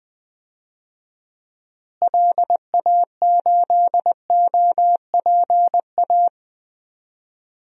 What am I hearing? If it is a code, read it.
Morse code, heard as a keyed tone, about 21 wpm, 700 Hz. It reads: LA8OPA
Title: DXCC-100 Call Signs - 6 Characters - Repeated 2x - 20wpm